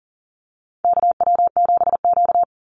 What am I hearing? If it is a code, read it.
Morse code, heard as a keyed tone, about 37 wpm, 700 Hz. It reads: KW7Q
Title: Call Signs Repeated 3x - 40wpm